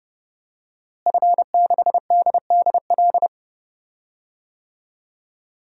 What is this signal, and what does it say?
Morse code, heard as a keyed tone, about 30 wpm, 700 Hz. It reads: F6DDL
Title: DXCC-100 Call Signs - 5 Characters - Repeated 3x - 30wpm